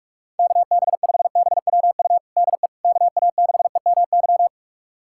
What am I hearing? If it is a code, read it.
Morse code, heard as a keyed tone, about 46 wpm, 700 Hz. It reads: KB5BWU DE KA6EMY
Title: USA Callsign DE Callsign - 6 Characters - 45wpm